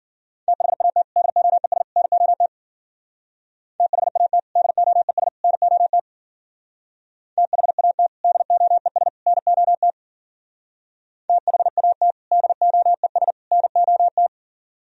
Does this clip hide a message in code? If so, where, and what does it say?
Yes: Morse code from 0.5 to 14.3 s, heard as a keyed tone, about 51 wpm, 700 Hz. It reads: THAT DOES NOT THAT DOES NOT THAT DOES NOT THAT DOES NOT